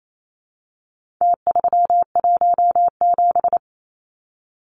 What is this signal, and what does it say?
Morse code, heard as a keyed tone, about 28 wpm, 700 Hz. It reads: T317
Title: Numbers - Contest Style - 4-Digit 28wpm